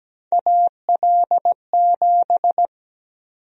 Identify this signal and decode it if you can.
Morse code, heard as a keyed tone, about 17 wpm, 700 Hz. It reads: AL7
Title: USA Call Sign Prefixes - 17wpm